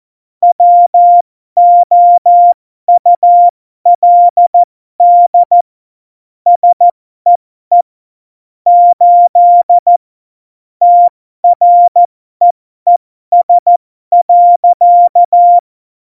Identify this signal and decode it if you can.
Morse code, heard as a keyed tone, about 14 wpm, spaced as for 12 wpm, 700 Hz. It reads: WOULD SEE 8 TREES.
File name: Morsekurs / Witze